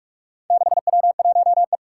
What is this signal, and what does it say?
Morse code, heard as a keyed tone, about 44 wpm, 700 Hz. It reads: 6W1E
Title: Four Challenging Letter-Number Combinations 45wpm